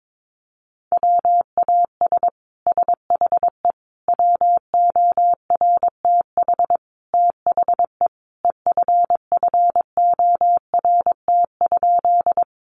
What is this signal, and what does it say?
Morse code, heard as a keyed tone, about 22 wpm, 700 Hz. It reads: WAS SHE WORTH THE EFFORT?
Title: Sentences from Top 1200 Words 22wpm